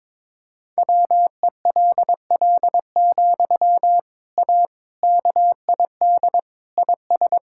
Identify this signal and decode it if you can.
Morse code, heard as a keyed tone, about 22 wpm, 700 Hz. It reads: WELL, A KID IS
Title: Sentences from Top 1600 Words 22wpm